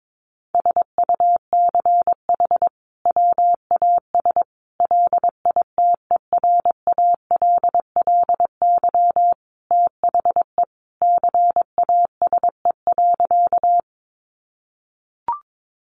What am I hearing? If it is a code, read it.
Morse code, heard as a keyed tone, about 22 wpm, 700 Hz. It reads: SUCH WAS LITERALLY THE CASE.
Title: Sentences from Top 1400 Words 22wpm